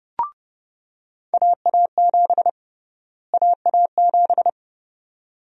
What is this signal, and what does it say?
Morse code, heard as a keyed tone, about 30 wpm, 700 Hz. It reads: AA7 AA7
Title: USA Call Sign Prefixes - Repeated 2x - 30wpm